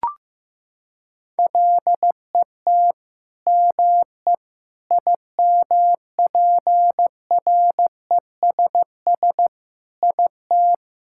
Morse code, heard as a keyed tone, about 16 wpm, 700 Hz. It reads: LET ME IMPRESS IT
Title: Sentences from Top 1600 Words 15wpm